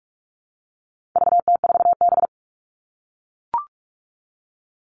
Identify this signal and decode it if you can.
Morse code, heard as a keyed tone, about 44 wpm, 700 Hz. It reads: VT4B
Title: All Cut Numbers - 4-Digit 45wpm